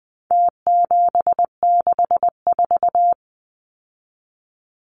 Morse code, heard as a keyed tone, about 20 wpm, 700 Hz. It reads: T764
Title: Numbers - Contest Style - 4-Digit 20wpm